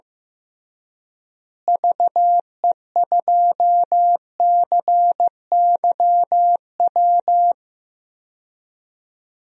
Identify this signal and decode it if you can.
Morse code, heard as a keyed tone, about 15 wpm, 700 Hz. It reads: VE2CYW
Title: DXCC-100 Call Signs - 6 Characters - Repeated 3x - 15wpm